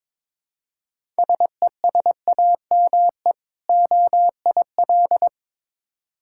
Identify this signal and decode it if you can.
Morse code, heard as a keyed tone, about 21 wpm, 690 Hz. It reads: SESAME OIL
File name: Difficult Words with Hints - 22wpm